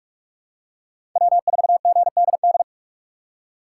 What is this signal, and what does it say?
Morse code, heard as a keyed tone, about 43 wpm, 700 Hz. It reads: W4GDD
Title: USA Call Signs - 5 Characters - 4x Character Spacing - 45wpm